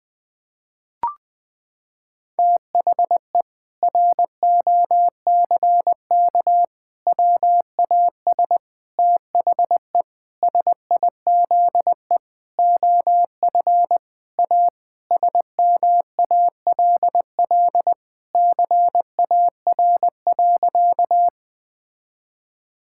Morse code, heard as a keyed tone, about 20 wpm, 700 Hz. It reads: THE ROCK WAS THE SIZE OF A SMALL CAR.
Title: Sentences with 4-Word Phrases - 20wpm